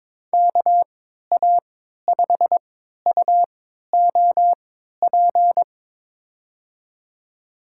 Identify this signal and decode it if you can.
Morse code, heard as a keyed tone, about 22 wpm, spaced as for 12 wpm, 700 Hz. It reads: KA5UOP
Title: USA Call Signs - 6 Characters - 3x Character Spacing - 22wpm